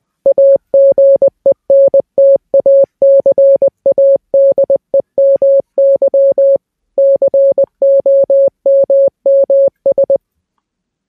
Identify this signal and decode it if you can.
Morse code, heard as a keyed tone, about 20 wpm, 550 Hz. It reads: AGENT ACADEMY COMMS